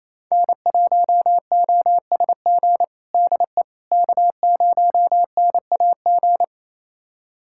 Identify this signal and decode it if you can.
Morse code, heard as a keyed tone, about 28 wpm, 700 Hz. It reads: N1OSG DE K0NAG
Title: USA Callsign DE Callsign - 5 Characters - 28wpm